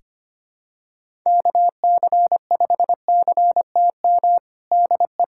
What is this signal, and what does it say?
Morse code, heard as a keyed tone, about 25 wpm, 700 Hz. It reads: KC5CTM DE
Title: USA Callsign DE Callsign - 6 Characters - 25wpm